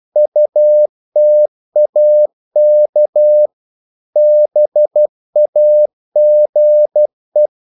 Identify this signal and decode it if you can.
Morse code, heard as a keyed tone, about 12 wpm, 600 Hz. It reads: UTAK BAGE
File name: Korte ord DK 12wpm